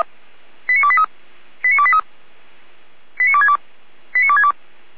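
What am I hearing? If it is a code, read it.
Five-tone selective call (ZVEI), heard as quick runs of five tones, tones of about 70 ms. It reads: 89283 89283 89262 89262
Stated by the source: ZVEI five-tone selective call received by SDR, logged by feuerwehr-orlamuende.org, a fire brigade.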